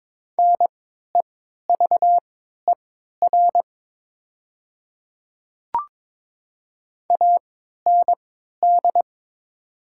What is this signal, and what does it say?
Morse code, heard as a keyed tone, about 22 wpm, spaced as for 13 wpm, 700 Hz. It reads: NEVER AND
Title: 3x Character Spacing - Top 500 Words - 22wpm